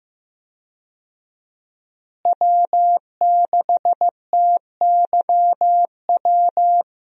Morse code, heard as a keyed tone, about 15 wpm, 700 Hz. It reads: W6TYW
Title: USA Call Signs - 5 Characters - Repeated 2x - 15wpm